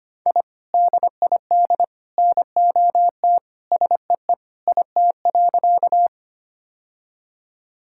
Morse code, heard as a keyed tone, about 25 wpm, 700 Hz. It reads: I DID NOT SEE IT.
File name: practice / 25wpm / Sentences with 3-Word Phrases - 25wpm